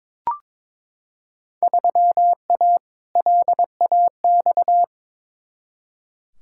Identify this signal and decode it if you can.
Morse code, heard as a keyed tone, about 22 wpm, 700 Hz. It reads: 3A LAX
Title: Field Day Exchanges - 22wpm